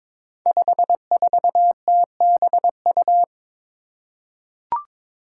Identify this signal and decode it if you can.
Morse code, heard as a keyed tone, about 22 wpm, 700 Hz. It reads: 54TBU
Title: Five Challenging Letter-Number Combinations 22wpm